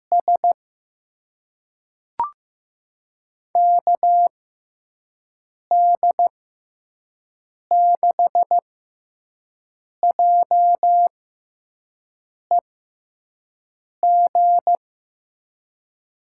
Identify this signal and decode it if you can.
Morse code, heard as a keyed tone, about 15 wpm, spaced as for 5 wpm, 700 Hz. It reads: S KD6JEG
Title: USA Call Signs - 6 Characters - 6x Character Spacing - 15wpm